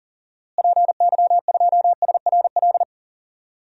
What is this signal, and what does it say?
Morse code, heard as a keyed tone, about 40 wpm, 700 Hz. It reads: PY2SRL
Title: DXCC-100 Call Signs - 6 Characters - Repeated 2x - 40wpm